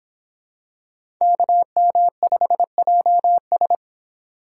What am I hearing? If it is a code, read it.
Morse code, heard as a keyed tone, about 24 wpm, 700 Hz. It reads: KM5JS